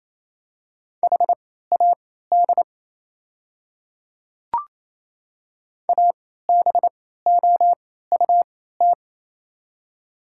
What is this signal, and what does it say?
Morse code, heard as a keyed tone, about 28 wpm, spaced as for 16 wpm, 690 Hz. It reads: HAD ABOUT
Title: 3x Character Spacing - Top 100 Words - 28wpm